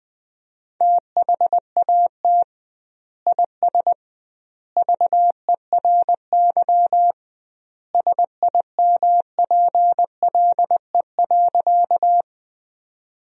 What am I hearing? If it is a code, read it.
Morse code, heard as a keyed tone, about 20 wpm, 700 Hz. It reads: THAT IS VERY SIMPLE.
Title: 2x Word-Spacing - Sentences from Top 400 Words - 20wpm